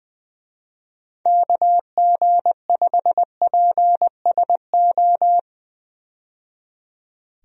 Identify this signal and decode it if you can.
Morse code, heard as a keyed tone, about 20 wpm, 700 Hz. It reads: KG5PSO
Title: USA Call Signs - 6 Characters - 20wpm